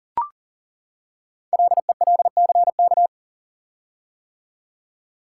Morse code, heard as a keyed tone, about 40 wpm, 700 Hz. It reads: LELCK
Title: Code Groups - 5 Letters - 40wpm